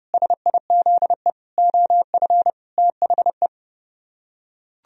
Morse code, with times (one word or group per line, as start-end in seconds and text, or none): 0.1-1.3 SIZE
1.6-2.5 OF
2.8-3.5 THE